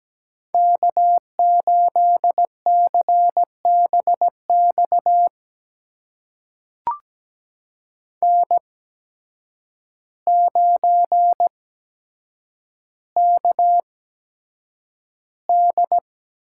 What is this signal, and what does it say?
Morse code, heard as a keyed tone, about 17 wpm, spaced as for 8 wpm, 700 Hz. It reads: K8CBX N9KD
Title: USA Call Signs - 8x Character Spacing - 17wpm